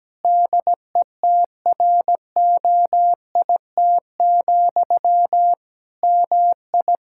Morse code, heard as a keyed tone, about 17 wpm, 700 Hz. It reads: DETROIT, MI
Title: Practice QSO - Exchange 1 - Beginning - 17wpm